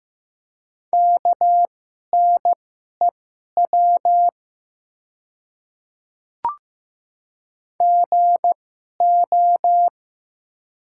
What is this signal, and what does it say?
Morse code, heard as a keyed tone, about 15 wpm, spaced as for 11 wpm, 700 Hz. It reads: KNEW GO
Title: 2x Character Spacing - Top 500 Words - 15wpm